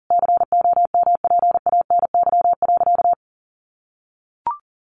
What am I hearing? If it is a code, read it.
Morse code, heard as a keyed tone, about 40 wpm, 700 Hz. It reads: COMPANY.
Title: Sentences from Top 1100 Words 40wpm